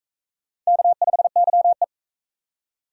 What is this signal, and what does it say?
Morse code, heard as a keyed tone, about 39 wpm, 700 Hz. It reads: K5YE